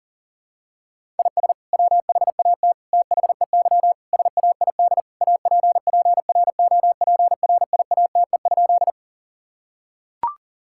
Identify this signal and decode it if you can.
Morse code, heard as a keyed tone, about 40 wpm, 700 Hz. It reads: IS WHAT THEY SAID APPROPRIATE?
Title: Sentences from Top 1000 Words - Review Part 5 - 40wpm